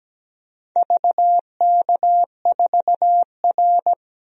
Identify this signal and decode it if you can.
Morse code, heard as a keyed tone, about 17 wpm, 700 Hz. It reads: VK4R
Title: DXCC-100 Call Signs - 4 Characters - 17wpm